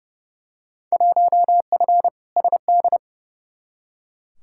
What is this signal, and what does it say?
Morse code, heard as a keyed tone, about 30 wpm, 700 Hz. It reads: 1F SD
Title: Field Day Exchanges - 30wpm